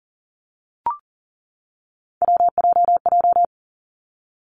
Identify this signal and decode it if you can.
Morse code, heard as a keyed tone, about 39 wpm, 700 Hz. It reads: WJJ
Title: Three Challenging Letter-Number Combinations 40wpm